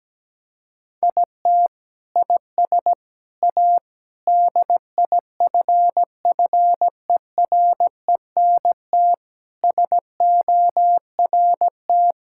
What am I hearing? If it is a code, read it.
Morse code, heard as a keyed tone, about 17 wpm, 700 Hz. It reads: IT IS A DIFFERENT SORT
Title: Sentences from Top 1000 Words - Review Part 5 - 17wpm